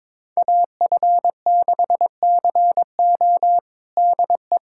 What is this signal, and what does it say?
Morse code, heard as a keyed tone, about 22 wpm, 700 Hz. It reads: AF6CO DE
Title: USA Callsign DE Callsign 22wpm